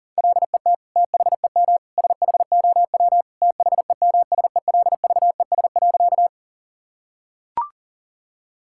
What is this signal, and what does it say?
Morse code, heard as a keyed tone, about 40 wpm, 700 Hz. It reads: LET THEM SHOW THEMSELVES.
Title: Sentences from Top 1000 Words - Review Part 5 - 40wpm